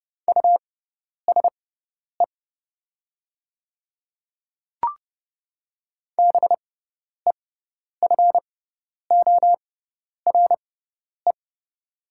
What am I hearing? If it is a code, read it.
Morse code, heard as a keyed tone, about 31 wpm, spaced as for 10 wpm, 690 Hz. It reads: USE BEFORE
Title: 6x Character Spacing - Top 200 Words - 30wpm